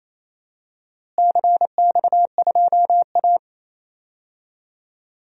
Morse code, heard as a keyed tone, about 28 wpm, 700 Hz. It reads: CX2A
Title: DXCC-100 Call Signs - 4 Characters - Repeated 3x - 28wpm